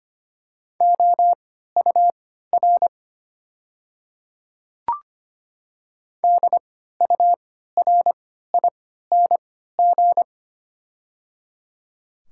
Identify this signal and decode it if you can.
Morse code, heard as a keyed tone, about 25 wpm, spaced as for 14 wpm, 700 Hz. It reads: OUR DURING
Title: 3x Character Spacing - Top 500 Words - 25wpm